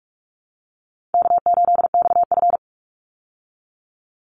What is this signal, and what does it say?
Morse code, heard as a keyed tone, about 44 wpm, 700 Hz. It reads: K8XF
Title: Easy Call Signs Repeated 2x - 45wpm